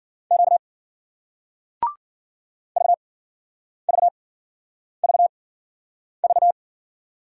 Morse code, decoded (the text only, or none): Q VVVV